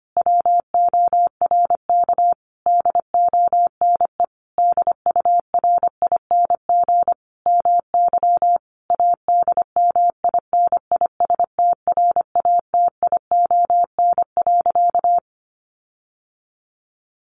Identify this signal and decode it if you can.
Morse code, heard as a keyed tone, about 25 wpm, 700 Hz. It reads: WORK DONE DURING MY ADMINISTRATION.
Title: Sentences from Top 1300 Words 25wpm